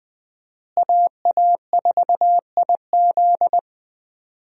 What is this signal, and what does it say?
Morse code, heard as a keyed tone, about 20 wpm, 700 Hz. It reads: AA4IZ
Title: USA Call Signs - 5 Characters - Repeated 3x - 20wpm